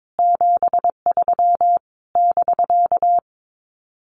Morse code, heard as a keyed tone, about 22 wpm, 700 Hz. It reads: 73 <BK>
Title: Common QSO Phrases Encore 22wpm